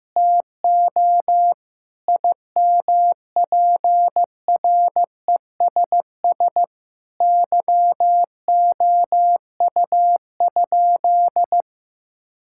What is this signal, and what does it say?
Morse code, heard as a keyed tone, about 15 wpm, 700 Hz. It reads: TO IMPRESS YOU?
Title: Sentences from Top 1600 Words 15wpm